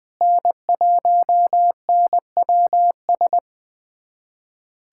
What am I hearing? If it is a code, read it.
Morse code, heard as a keyed tone, about 20 wpm, 700 Hz. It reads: N1NWS
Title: USA Callsign DE Callsign - 5 Characters - 20wpm